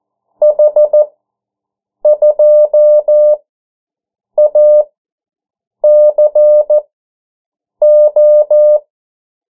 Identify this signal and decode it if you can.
Morse code, heard as a keyed tone, about 14 wpm, spaced as for 7 wpm, 600 Hz. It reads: H2ACO
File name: morse mp3 / 09 urok 8